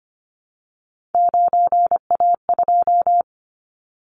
Morse code, heard as a keyed tone, about 25 wpm, 700 Hz. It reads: 9A2
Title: Callsign Prefixes for Top 100 DXCC - 25wpm